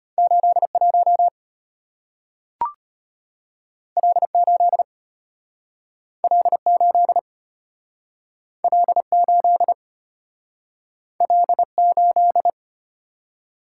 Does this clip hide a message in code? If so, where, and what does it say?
Yes: Morse code from 0.2 to 12.5 s, heard as a keyed tone, about 34 wpm, 700 Hz. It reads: 81 L8 L8 L8 L8